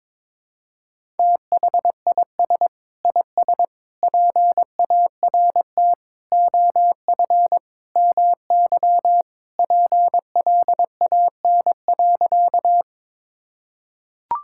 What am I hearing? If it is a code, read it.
Morse code, heard as a keyed tone, about 22 wpm, 700 Hz. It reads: THIS IS PART OF MY PLAN.
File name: practice / 22wpm / Sentences with 2-Word Phrases - 22wpm